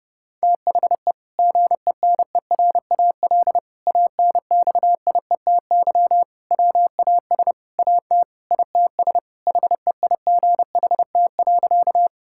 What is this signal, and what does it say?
Morse code, heard as a keyed tone, about 31 wpm, 700 Hz. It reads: THE GENERAL ANXIETY WAS AT ITS HEIGHT.
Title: Sentences from Top 1600 Words 30wpm